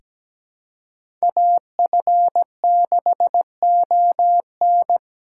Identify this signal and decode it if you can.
Morse code, heard as a keyed tone, about 17 wpm, 700 Hz. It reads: AF6ON
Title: USA Call Signs - 8x Character Spacing - 17wpm